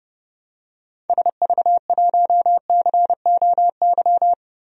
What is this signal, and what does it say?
Morse code, heard as a keyed tone, about 30 wpm, 690 Hz. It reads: SV1COY
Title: DXCC-100 Call Signs - 6 Characters - 30wpm